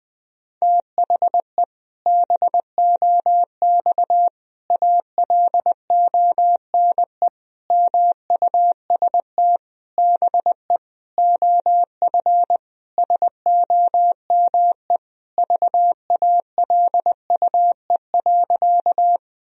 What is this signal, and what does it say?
Morse code, heard as a keyed tone, about 20 wpm, 700 Hz. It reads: THE BOX ALONE MUST BE OF SOME VALUE.
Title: Sentences from Top 1000 Words - Review Part 3 - 20wpm